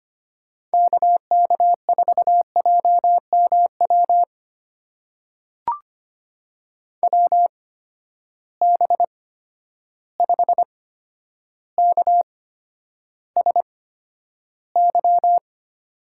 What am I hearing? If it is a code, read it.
Morse code, heard as a keyed tone, about 25 wpm, spaced as for 11 wpm, 700 Hz. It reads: KK4JMW WB5KSY
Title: USA Call Signs - 6 Characters - 8x Character Spacing - 25wpm